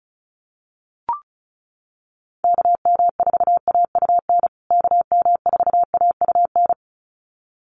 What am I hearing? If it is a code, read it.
Morse code, heard as a keyed tone, about 34 wpm, 700 Hz. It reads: KM4AUN KM4AUN
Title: USA Call Signs Repeated 2x 35wpm